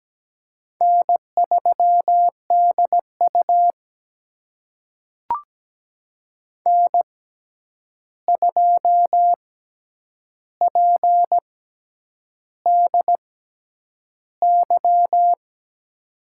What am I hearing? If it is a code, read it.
Morse code, heard as a keyed tone, about 17 wpm, spaced as for 8 wpm, 700 Hz. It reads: N3DU N2PDY
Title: USA Call Signs - 6x Character Spacing - 17wpm